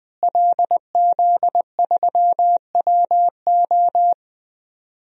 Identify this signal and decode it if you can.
Morse code, heard as a keyed tone, about 21 wpm, 700 Hz. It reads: LZ3WO
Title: DXCC-100 Call Signs - 5 Characters - Repeated 2x - 20wpm